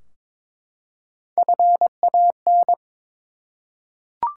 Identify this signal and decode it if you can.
Morse code, heard as a keyed tone, about 23 wpm, 700 Hz. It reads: FAN
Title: Three Letter Words - 22wpm